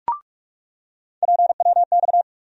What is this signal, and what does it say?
Morse code, heard as a keyed tone, about 46 wpm, 700 Hz. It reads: PWX
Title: USA Call Sign Postfixes - 3 Letters 45wpm